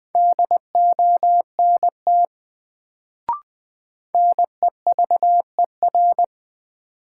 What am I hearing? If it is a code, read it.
Morse code, heard as a keyed tone, about 20 wpm, 700 Hz. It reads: DONT NEVER
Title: Sentences from Top 300 Words - New Word Drill - 20wpm